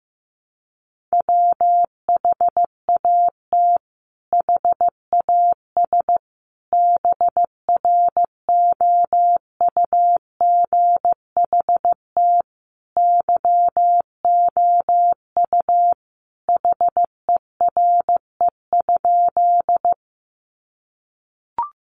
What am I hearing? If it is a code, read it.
Morse code, heard as a keyed tone, about 15 wpm, 700 Hz. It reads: WHAT HAS BROUGHT YOU HERE?
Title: Sentences from Top 500 Words 15wpm